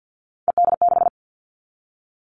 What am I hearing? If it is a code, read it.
Morse code, heard as a keyed tone, about 50 wpm, 700 Hz. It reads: ED6
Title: Callsign Prefixes for Top 50 DXCC - 50wpm